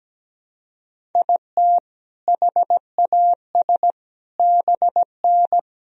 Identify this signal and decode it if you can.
Morse code, heard as a keyed tone, about 17 wpm, 700 Hz. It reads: IT HAS BN
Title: Sentences from Top 400 Words and Abbreviations 17wpm